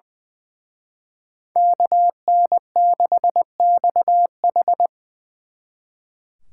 Morse code, heard as a keyed tone, about 24 wpm, 700 Hz. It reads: KN6XH
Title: Speed Racing - USA Call Signs 20wpm